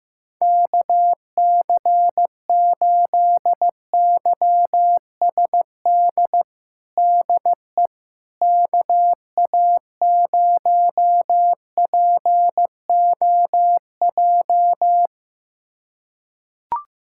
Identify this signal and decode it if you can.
Morse code, heard as a keyed tone, about 16 wpm, 700 Hz. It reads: KC8YSD DE KA0POJ
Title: USA Callsign DE Callsign - 6 Characters - 15wpm